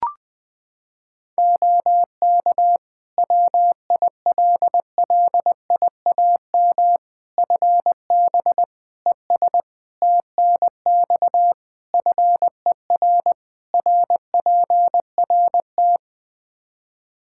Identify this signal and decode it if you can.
Morse code, heard as a keyed tone, about 20 wpm, 700 Hz. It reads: OK WILLIAM FB ES TNX FER RPRT
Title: Practice QSO - Exchange 2 - Info - 20wpm